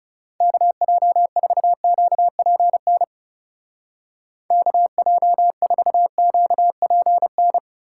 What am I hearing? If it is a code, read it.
Morse code, heard as a keyed tone, about 38 wpm, 700 Hz. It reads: KJ4QPN KJ4QPN